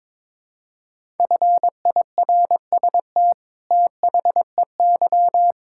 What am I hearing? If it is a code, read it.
Morse code, heard as a keyed tone, about 22 wpm, 700 Hz. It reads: FIRST THEY
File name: practice / 22wpm / Sets of 8 Words - 22wpm